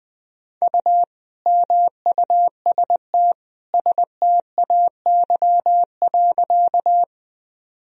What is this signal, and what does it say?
Morse code, heard as a keyed tone, about 21 wpm, 700 Hz. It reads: U MUST STAY.
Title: Sentences from Top 500 Words and Abbreviations 20wpm